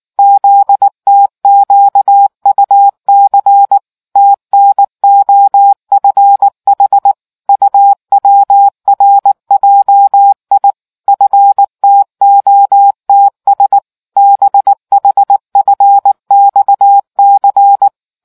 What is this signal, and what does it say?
Morse code, heard as a keyed tone, about 19 wpm, 800 Hz. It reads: ZTQUC TNOFH UWRJI FTOTS BHFXC